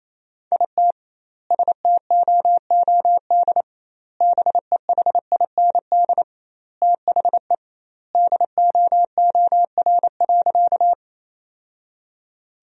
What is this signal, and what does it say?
Morse code, heard as a keyed tone, about 28 wpm, 700 Hz. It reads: IT STOOD BEHIND THE DOOR.
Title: 2x Word-Spacing - Sentences from Top 500 Words - 28wpm